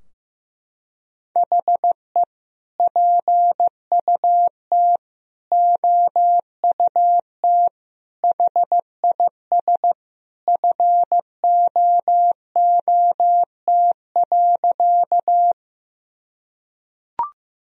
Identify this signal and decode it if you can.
Morse code, heard as a keyed tone, about 15 wpm, 700 Hz. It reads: HE PUT OUT HIS FOOT.
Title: Sentences from Top 1000 Words - Review Part 3 - 15wpm